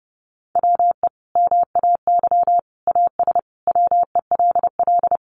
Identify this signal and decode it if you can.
Morse code, heard as a keyed tone, about 30 wpm, 700 Hz. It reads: WE MAY AS WELL
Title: Sentences from Top 800 Words 30wpm